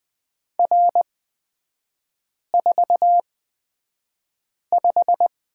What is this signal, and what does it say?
Morse code, heard as a keyed tone, about 20 wpm, spaced as for 5 wpm, 700 Hz. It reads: R45
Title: 224 Lesson 36e - BT - Three Characters 20wpm